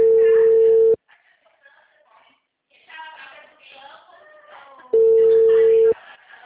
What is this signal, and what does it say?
Ringback tone (China, 450 Hz), with speech.
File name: out- -2002- - - .34